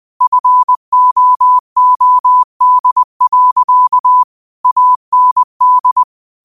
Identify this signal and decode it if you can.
Morse code, heard as a keyed tone, about 20 wpm, 1000 Hz. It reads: FOOD. AND
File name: qotdfiles / 20wpm